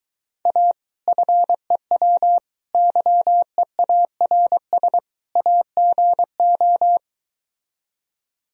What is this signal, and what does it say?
Morse code, heard as a keyed tone, about 23 wpm, 700 Hz. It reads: A FEW YEARS AGO